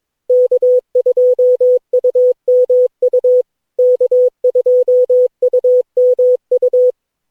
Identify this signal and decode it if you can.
Morse code, heard as a keyed tone, about 21 wpm, 500 Hz. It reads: K2UMU K2UMU
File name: cw / Session9 cw training